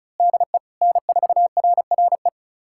Morse code, heard as a keyed tone, about 36 wpm, 700 Hz. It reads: DE N4RRE
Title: USA Callsign DE Callsign - 5 Characters - 35wpm